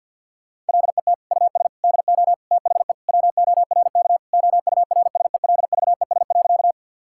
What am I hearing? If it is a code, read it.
Morse code, heard as a keyed tone, about 51 wpm, 700 Hz. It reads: LET US DO THE WORK OURSELVES.
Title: Sentences from Top 1600 Words 50wpm